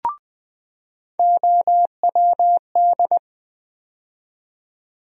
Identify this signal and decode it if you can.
Morse code, heard as a keyed tone, about 20 wpm, 700 Hz. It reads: OWD
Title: USA Call Sign Postfixes - 3 Letters 20wpm